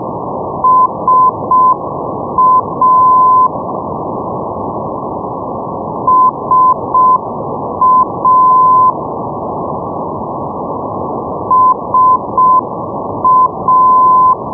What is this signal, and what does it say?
Morse code, heard as a keyed tone, about 6 wpm, 1020 Hz, with noise. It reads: SA SA SA
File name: europe / pol / pol-531 ndb darlowo sa 474.5 (rug) 2008 rb